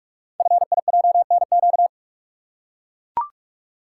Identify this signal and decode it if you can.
Morse code, heard as a keyed tone, about 44 wpm, 690 Hz. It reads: FIJNQ
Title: Code Groups - 5 Letters - 45wpm